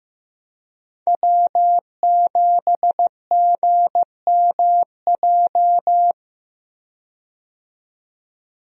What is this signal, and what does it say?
Morse code, heard as a keyed tone, about 15 wpm, 700 Hz. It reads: W7GMJ
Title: USA Call Signs - 5 Characters - 15wpm